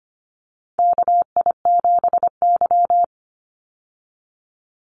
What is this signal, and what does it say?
Morse code, heard as a keyed tone, about 25 wpm, 700 Hz. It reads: KI7Y
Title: Call Signs Repeated 2x - 25wpm